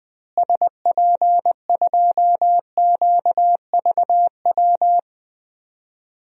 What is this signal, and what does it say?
Morse code, heard as a keyed tone, about 21 wpm, 700 Hz. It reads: SP2QVW
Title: DXCC-100 Call Signs - 6 Characters - Repeated 3x - 20wpm